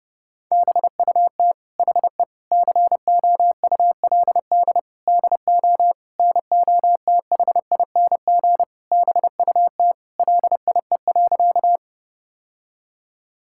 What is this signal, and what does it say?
Morse code, heard as a keyed tone, about 30 wpm, 700 Hz. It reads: BUT HE COULD DO NOTHING BUT LIE.
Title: Sentences from Top 1600 Words 30wpm